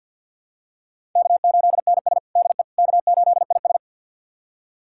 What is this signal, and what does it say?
Morse code, heard as a keyed tone, about 48 wpm, 690 Hz. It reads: K8NS DE K8IS
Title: USA Callsign DE Callsign - 4 Characters - 50wpm